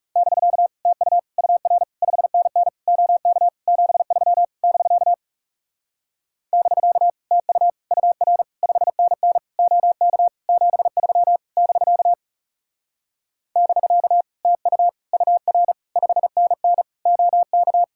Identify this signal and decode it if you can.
Morse code, heard as a keyed tone, about 37 wpm, 700 Hz. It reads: <BK> TU UR 5NN OK 73 <BK> <BK> TU UR 5NN OK 73 <BK> <BK> TU UR 5NN OK
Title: Speed-Racing Common QSO Phrases - 30wpm